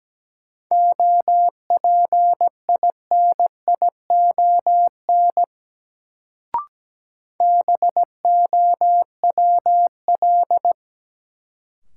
Morse code, heard as a keyed tone, about 17 wpm, 700 Hz. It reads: OPINION BOWL
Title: Sentences from Top 1200 Words - New Word Drill Alt - 17wpm